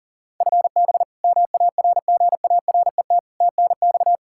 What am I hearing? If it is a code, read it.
Morse code, heard as a keyed tone, about 40 wpm, 700 Hz. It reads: FB MARGARET TNX
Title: Practice QSO - Exchange 3 - More Info - 40wpm